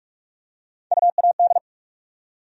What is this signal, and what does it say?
Morse code, heard as a keyed tone, about 42 wpm, 700 Hz. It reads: UAD
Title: USA Call Sign Postfixes - 3 Letters 45wpm